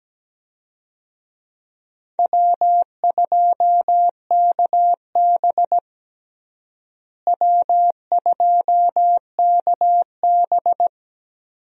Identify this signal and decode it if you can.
Morse code, heard as a keyed tone, about 17 wpm, 700 Hz. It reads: W2KB W2KB
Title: USA Call Signs - 4 Characters - Repeated 2x - 17wpm